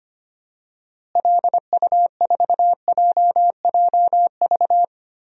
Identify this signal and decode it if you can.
Morse code, heard as a keyed tone, about 25 wpm, 700 Hz. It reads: LU4JJV
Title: DXCC-100 Call Signs - 6 Characters - Repeated 3x - 25wpm